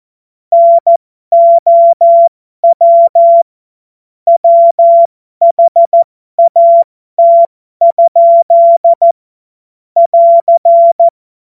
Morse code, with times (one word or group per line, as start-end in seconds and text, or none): 0.5-3.4 NOW
4.3-9.1 WHAT?
10.0-11.1 <AR>